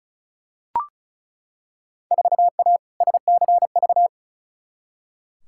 Morse code, heard as a keyed tone, about 36 wpm, 700 Hz. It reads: 4A SCV
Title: Field Day Exchanges - 35wpm